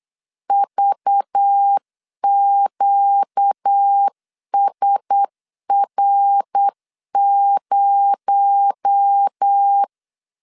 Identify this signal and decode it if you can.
Morse code, heard as a keyed tone, about 8 wpm, 800 Hz. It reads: VQSR0
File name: CW / DET / 1 6 1500C S C 8wpm 5219